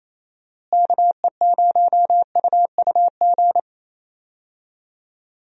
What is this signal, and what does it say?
Morse code, heard as a keyed tone, about 28 wpm, 700 Hz. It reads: KE0UUG
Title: USA Call Signs - 6 Characters - Repeated 2x - 28wpm